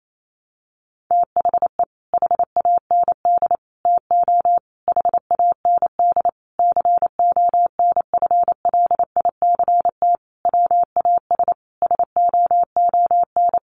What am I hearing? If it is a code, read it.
Morse code, heard as a keyed tone, about 28 wpm, 700 Hz. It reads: THE HAND TO HAND CONFLICT WAS SOON
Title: Sentences from Top 1300 Words 28wpm